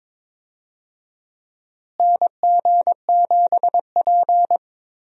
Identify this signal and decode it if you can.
Morse code, heard as a keyed tone, about 22 wpm, 700 Hz. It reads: NG7P
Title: USA Call Signs - 4 Characters - Repeated 2x - 22wpm